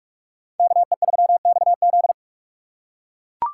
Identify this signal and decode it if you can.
Morse code, heard as a keyed tone, about 44 wpm, 700 Hz. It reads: KE3XZ